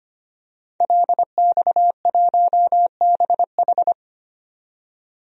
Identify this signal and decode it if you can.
Morse code, heard as a keyed tone, about 25 wpm, 700 Hz. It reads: LX1BH
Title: DXCC-100 Call Signs - 5 Characters - 25wpm